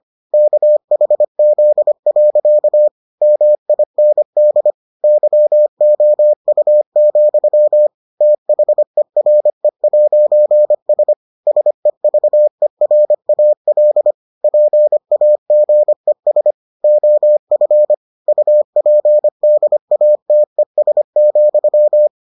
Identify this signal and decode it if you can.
Morse code, heard as a keyed tone, about 25 wpm, 600 Hz. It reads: KHZ. MIND YOU, THERE'S SEVERAL PAGES OF UPDATES,